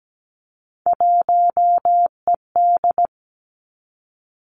Morse code, heard as a keyed tone, about 17 wpm, 700 Hz. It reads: 1ED
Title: Three Challenging Letter-Number Combinations 17wpm